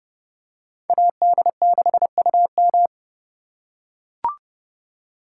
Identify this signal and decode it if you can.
Morse code, heard as a keyed tone, about 29 wpm, 700 Hz. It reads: AD6UM